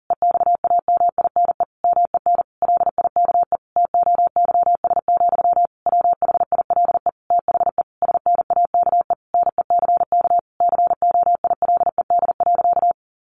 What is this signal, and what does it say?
Morse code, heard as a keyed tone, about 40 wpm, 700 Hz. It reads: EXAMINE MEN LIKE TOYS, WHILE THE SNAKE NECK COILED.
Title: A Sound Of Thunder 40wpm